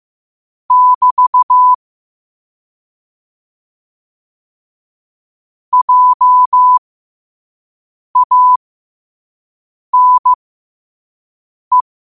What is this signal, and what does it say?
Morse code, heard as a keyed tone, about 15 wpm, spaced as for 5 wpm, 1000 Hz. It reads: <BT> JANE
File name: qotdfiles / 5wpm